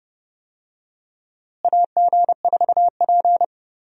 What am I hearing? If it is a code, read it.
Morse code, heard as a keyed tone, about 30 wpm, 700 Hz. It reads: AG4P
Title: USA Call Signs - 4 Characters - Repeated 3x - 30wpm